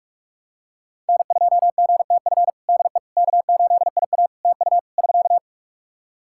Morse code, heard as a keyed tone, about 46 wpm, 700 Hz. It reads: N2GTF DE K8IA TU <SK>
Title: Practice QSO - Exchange 4 - Ending - 45wpm